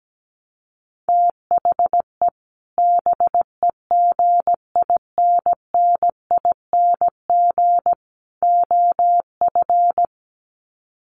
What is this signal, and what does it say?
Morse code, heard as a keyed tone, about 17 wpm, 700 Hz. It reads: THE BEGINNING OF
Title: Sets of 3 Words 17wpm